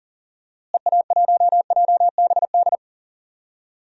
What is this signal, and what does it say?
Morse code, heard as a keyed tone, about 39 wpm, 700 Hz. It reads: EA1JBD
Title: DXCC-100 Call Signs - 6 Characters - Repeated 2x - 40wpm